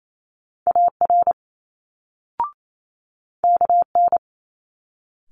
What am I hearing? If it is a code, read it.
Morse code, heard as a keyed tone, about 28 wpm, 700 Hz. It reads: AR KN
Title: Multi-letter Phonemes Quick Fire 28wpm